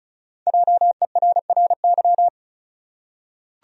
Morse code, heard as a keyed tone, about 34 wpm, 700 Hz. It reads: JERRY
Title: USA Common Names - Max 5 Letters - 35wpm